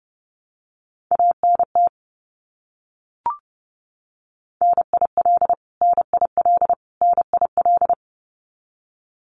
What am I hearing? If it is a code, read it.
Morse code, heard as a keyed tone, about 30 wpm, 700 Hz. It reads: ANT NIL NIL NIL
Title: Abbreviations Repeated 3x - 30wpm